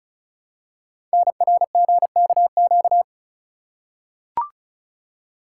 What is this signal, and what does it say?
Morse code, heard as a keyed tone, about 36 wpm, 700 Hz. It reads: NRGKQ
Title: Code Groups - 5 Letters - 35wpm